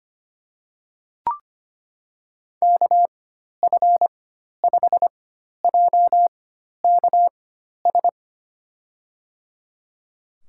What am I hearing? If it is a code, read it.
Morse code, heard as a keyed tone, about 25 wpm, spaced as for 12 wpm, 700 Hz. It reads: KF5JKS
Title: USA Call Signs - 4x Character Spacing - 25wpm